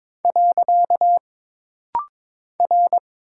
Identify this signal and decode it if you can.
Morse code, heard as a keyed tone, about 22 wpm, 700 Hz. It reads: . R
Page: This rapid-fire timing is geared to push your ICR (Instant Character Recognition) skills.